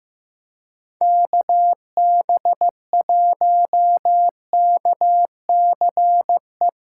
Morse code, heard as a keyed tone, about 16 wpm, 700 Hz. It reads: KB1KCE
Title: USA Callsign DE Callsign 15wpm